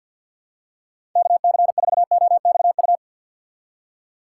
Morse code, heard as a keyed tone, about 52 wpm, 700 Hz. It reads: KC4OXU
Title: USA Call Signs - 6 Characters - 6x Character Spacing - 50wpm